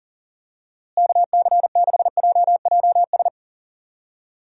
Morse code, heard as a keyed tone, about 37 wpm, 700 Hz. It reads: KC6JJS
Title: USA Call Signs - 6 Characters - Repeated 2x - 40wpm